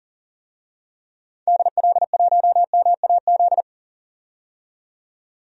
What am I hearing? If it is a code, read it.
Morse code, heard as a keyed tone, about 40 wpm, 700 Hz. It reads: DL1MAZ
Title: DXCC-100 Call Signs - 6 Characters - 40wpm